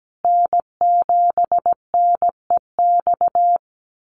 Morse code, heard as a keyed tone, about 17 wpm, 700 Hz. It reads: N7NEX
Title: USA Call Signs Repeated 2x 17wpm